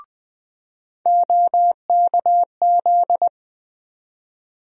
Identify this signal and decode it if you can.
Morse code, heard as a keyed tone, about 20 wpm, 700 Hz. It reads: OKZ
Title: USA Call Sign Postfixes - 3 Letters 20wpm